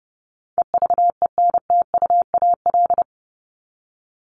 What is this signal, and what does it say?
Morse code, heard as a keyed tone, about 30 wpm, 700 Hz. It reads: EVENTUAL
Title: LY Suffix - Top 100 Words with Base Words 30wpm